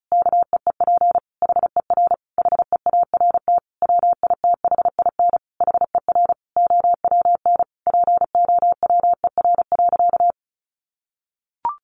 Morse code, heard as a keyed tone, about 34 wpm, 700 Hz. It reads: KEEP HER HEART WITHIN HER OWN POWER.
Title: Sentences from Top 1200 Words 35wpm